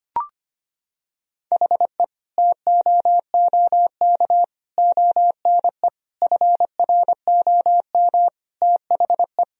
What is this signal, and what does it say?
Morse code, heard as a keyed tone, about 25 wpm, 700 Hz. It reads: HE TOOK ONE FROM THE
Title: Sentences from Top 1000 Words - Review Part 1 - 25wpm